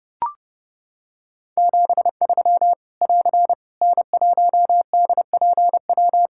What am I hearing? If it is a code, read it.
Morse code, heard as a keyed tone, about 30 wpm, 700 Hz. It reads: 73 <AR> N1DPW
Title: K1USN SST QSO Practice 30wpm